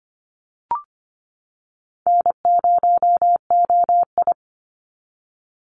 Morse code, heard as a keyed tone, about 25 wpm, 700 Hz. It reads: N0OI
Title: NA SOTA Activator Callsigns 25wpm